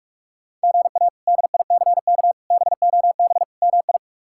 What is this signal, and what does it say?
Morse code, heard as a keyed tone, about 46 wpm, 700 Hz. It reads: GA DICK BOB MI
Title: K1USN SST QSO Practice - 45wpm